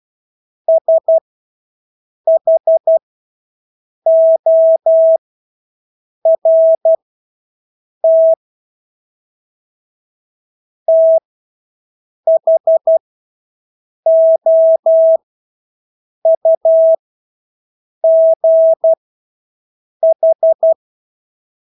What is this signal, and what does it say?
Morse code, heard as a keyed tone, about 12 wpm, spaced as for 6 wpm, 650 Hz. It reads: SHORT THOUGH
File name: Engelske ord 12-6 wpm